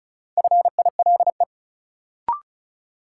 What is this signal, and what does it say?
Morse code, heard as a keyed tone, about 36 wpm, 700 Hz. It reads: FILE
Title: Sentences from Top 1200 Words - New Word Drill Alt - 35wpm